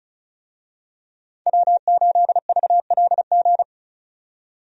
Morse code, heard as a keyed tone, about 36 wpm, 700 Hz. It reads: W8VLG
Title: USA Call Signs - 5 Characters - 35wpm